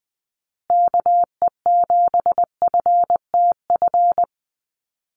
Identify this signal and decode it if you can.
Morse code, heard as a keyed tone, about 20 wpm, 700 Hz. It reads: KE7FTF
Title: USA Call Signs Repeated 2x 20wpm